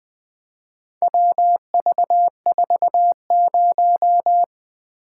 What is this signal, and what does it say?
Morse code, heard as a keyed tone, about 20 wpm, 700 Hz. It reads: WV40
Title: Four Letter-Number Combinations - 20wpm